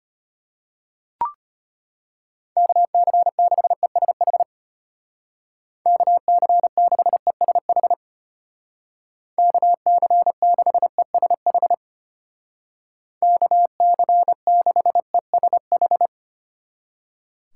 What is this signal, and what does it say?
Morse code, heard as a keyed tone, about 30 wpm, 700 Hz. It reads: KC6ESH KC6ESH KC6ESH KC6ESH
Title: Speed Racing - USA Call Signs 25wpm